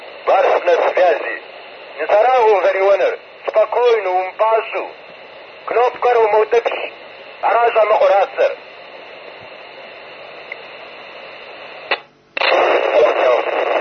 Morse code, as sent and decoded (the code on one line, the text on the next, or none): none